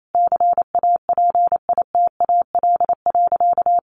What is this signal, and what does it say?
Morse code, heard as a keyed tone, about 28 wpm, 700 Hz. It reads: CAPITAL.
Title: Sentences from Top 1500 Words 28wpm